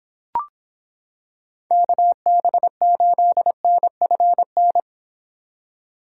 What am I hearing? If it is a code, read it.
Morse code, heard as a keyed tone, about 26 wpm, 700 Hz. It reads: KB8NFN